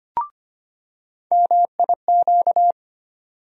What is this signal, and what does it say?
Morse code, heard as a keyed tone, about 25 wpm, 700 Hz. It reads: MIQ
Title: USA Call Sign Postfixes - 3 Letters 25wpm